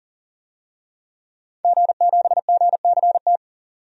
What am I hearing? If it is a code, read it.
Morse code, heard as a keyed tone, about 40 wpm, 700 Hz. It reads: G7GCT
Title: DXCC-100 Call Signs - 5 Characters - 40wpm